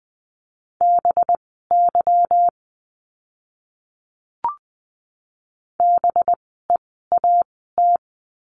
Morse code, heard as a keyed tone, about 20 wpm, spaced as for 14 wpm, 700 Hz. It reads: BY BEAT
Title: 185 Lesson 30f - Letter b - Words 20wpm